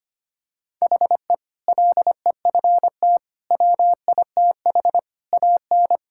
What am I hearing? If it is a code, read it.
Morse code, heard as a keyed tone, about 25 wpm, 700 Hz. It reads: HE LEFT WITH AN
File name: practice / 25wpm / Sentences with 2-Word Phrases - 25wpm